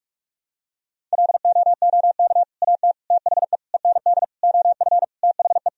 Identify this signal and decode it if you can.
Morse code, heard as a keyed tone, about 46 wpm, 700 Hz. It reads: LOOK AT THE END OF THE
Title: Sentences from Top 1000 Words - Review Part 1 - 45wpm